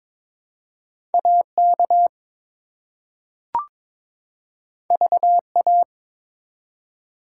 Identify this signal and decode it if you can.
Morse code, heard as a keyed tone, about 22 wpm, 700 Hz. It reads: AK VA
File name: ARRL Sections v2 - 22wpm